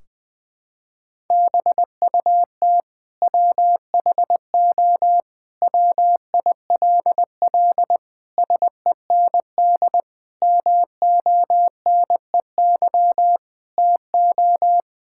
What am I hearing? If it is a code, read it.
Morse code, heard as a keyed tone, about 21 wpm, 700 Hz. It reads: BUT WHO WILL SEND MONEY TO
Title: Sentences from Top 1600 Words 20wpm